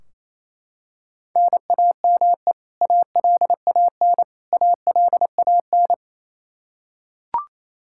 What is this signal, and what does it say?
Morse code, heard as a keyed tone, about 28 wpm, 700 Hz. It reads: NAME ALAN ALAN
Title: Practice QSO - Exchange 1 - Beginning - 28wpm